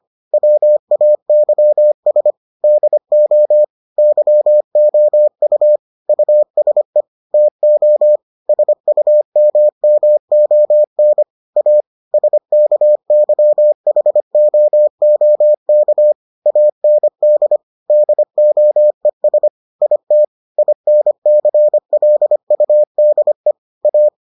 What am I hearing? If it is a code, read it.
Morse code, heard as a keyed tone, about 25 wpm, 600 Hz. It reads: WAYS DO YOU USE TO SUMMON A SKYHOOK AND DOES IT INCLUDE A